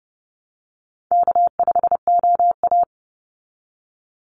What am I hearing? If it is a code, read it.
Morse code, heard as a keyed tone, about 30 wpm, 700 Hz. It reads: K5OA
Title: Call Signs Repeated 2x - 30wpm